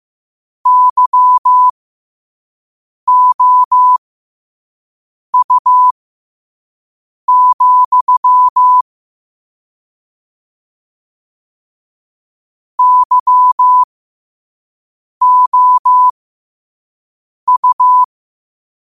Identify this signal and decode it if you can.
Morse code, heard as a keyed tone, about 15 wpm, spaced as for 5 wpm, 1000 Hz. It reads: YOU, YOU
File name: qotdfiles / 5wpm